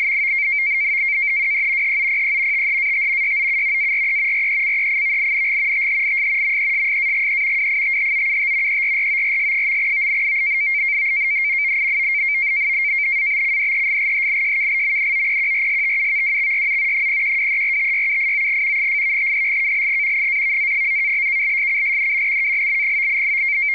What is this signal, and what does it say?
NAVTEX (SITOR-B), about 100 baud. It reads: WE HOPE THAT YOU ARE RECEIVING THIS MESSAGE WITHOUT ERRORS. ABCDEFGHIJKLMNOPQRSTUVWXYZ 0123456789 !=#$&() -=-+()()':',./()?